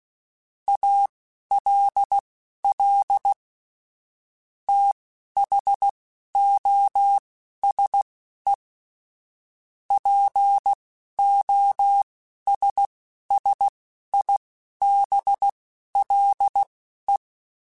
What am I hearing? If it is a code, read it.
Morse code, heard as a keyed tone, about 16 wpm, spaced as for 11 wpm, 800 Hz. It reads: ALL THOSE POSSIBLE
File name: Technical Subjects / Technical Subjects-mp3s / AND Gate @10